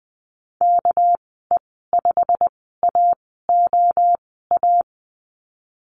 Morse code, heard as a keyed tone, about 20 wpm, spaced as for 15 wpm, 700 Hz. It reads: KE5AOA
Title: 207 Lesson 33g - Letter j - Call Signs 20wpm